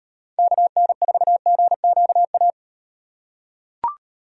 Morse code, heard as a keyed tone, about 38 wpm, 700 Hz. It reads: KN4GQA